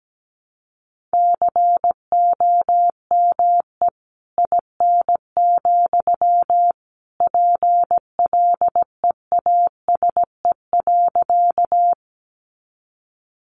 Morse code, heard as a keyed tone, about 17 wpm, 700 Hz. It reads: COME IN, PLEASE.
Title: Sentences from Top 1500 Words 17wpm